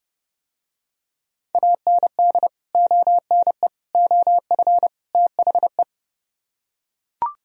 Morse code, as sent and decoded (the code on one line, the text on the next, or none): .- -. -.. / --- -. / --- ..-. / - .... .
AND ONE OF THE